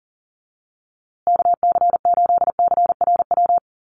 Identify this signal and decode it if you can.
Morse code, heard as a keyed tone, about 40 wpm, 700 Hz. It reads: KC8CRW
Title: USA Call Signs Repeated 2x 40wpm